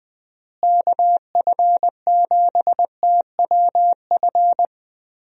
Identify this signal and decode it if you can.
Morse code, heard as a keyed tone, about 20 wpm, 700 Hz. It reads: KF7TWF
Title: USA Call Signs - 6 Characters - Repeated 3x - 20wpm